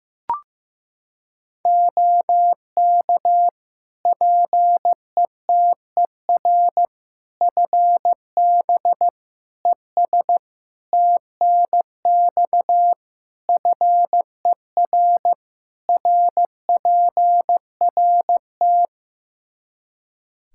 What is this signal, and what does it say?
Morse code, heard as a keyed tone, about 15 wpm, 700 Hz. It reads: OK PETER FB ES TNX FER RPRT
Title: K1USN SST QSO Practice 15wpm